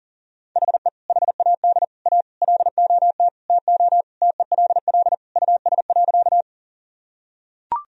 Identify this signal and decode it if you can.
Morse code, heard as a keyed tone, about 40 wpm, 700 Hz. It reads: HE HAD A LOT TO TELL US.
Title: Sentences from Top 1000 Words - Review Part 3 - 40wpm